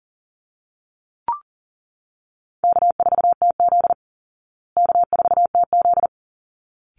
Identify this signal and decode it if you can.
Morse code, heard as a keyed tone, about 40 wpm, 700 Hz. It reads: K4TZ K4TZ
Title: Easy Call Signs Repeated 2x - 40wpm